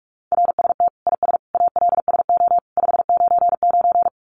Morse code, heard as a keyed tone, about 44 wpm, 700 Hz. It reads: RST IS ALSO 599
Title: Common QSO Phrases 45wpm